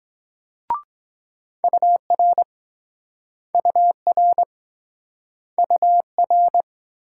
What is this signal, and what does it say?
Morse code, heard as a keyed tone, about 23 wpm, 700 Hz. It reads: UR UR UR